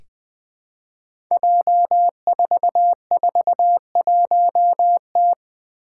Morse code, heard as a keyed tone, about 20 wpm, 700 Hz. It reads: J441T
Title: Five Challenging Letter-Number Combinations 20wpm